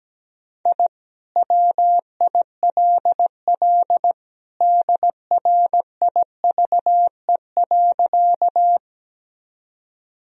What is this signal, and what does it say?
Morse code, heard as a keyed tone, about 17 wpm, 700 Hz. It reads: I WILL DRIVE.
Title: Sentences from Top 1000 Words - Review Part 2 - 17wpm